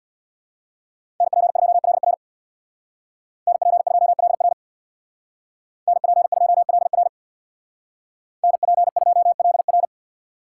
Morse code, heard as a keyed tone, about 64 wpm, 700 Hz. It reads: NP2LR NP2LR NP2LR NP2LR